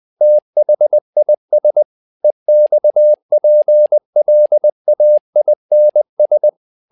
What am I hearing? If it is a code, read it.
Morse code, heard as a keyed tone, about 20 wpm, 600 Hz. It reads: THIS EXPLAINS
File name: tower trouble 20wpm